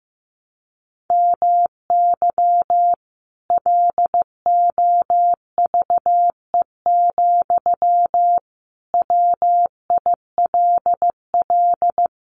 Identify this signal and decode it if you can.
Morse code, heard as a keyed tone, about 15 wpm, 700 Hz. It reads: MY LOVE, WILL
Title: Sentences from Top 1200 Words 15wpm